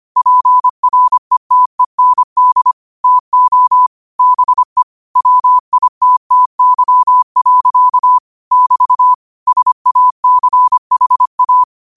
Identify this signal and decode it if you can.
Morse code, heard as a keyed tone, about 25 wpm, 1000 Hz. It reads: PRETEND TO BE WITTY. <BT> SACHA